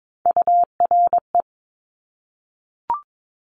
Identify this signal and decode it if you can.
Morse code, heard as a keyed tone, about 22 wpm, 700 Hz. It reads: URE
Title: Top Three Letter Combinations in Words 22wpm